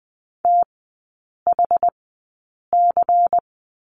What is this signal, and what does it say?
Morse code, heard as a keyed tone, about 20 wpm, spaced as for 8 wpm, 700 Hz. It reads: THC
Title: 093 Lesson 16e - Letter c - Three Characters 20wpm